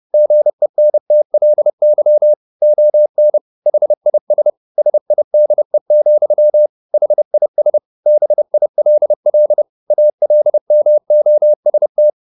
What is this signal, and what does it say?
Morse code, heard as a keyed tone, about 30 wpm, 600 Hz. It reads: GENTLY ON HIS SIDE, HIS BILL ALMOST